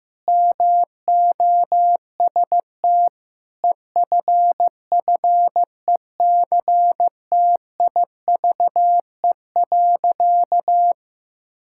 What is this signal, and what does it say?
Morse code, heard as a keyed tone, about 16 wpm, 700 Hz. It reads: MOST EFFECTIVE.
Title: Sentences from Top 1000 Words - Review Part 6 - 15wpm